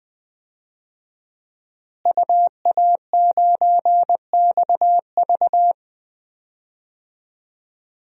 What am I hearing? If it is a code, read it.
Morse code, heard as a keyed tone, about 20 wpm, 700 Hz. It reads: UA9XV
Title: DXCC-100 Call Signs - 5 Characters - Repeated 3x - 20wpm